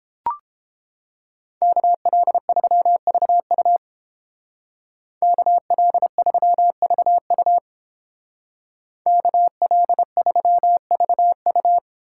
Morse code, heard as a keyed tone, about 30 wpm, 700 Hz. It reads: KL3VU KL3VU KL3VU